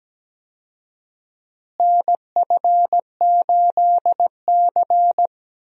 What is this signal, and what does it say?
Morse code, heard as a keyed tone, about 17 wpm, 700 Hz. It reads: NF8C
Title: USA Call Signs - 4 Characters - Repeated 3x - 17wpm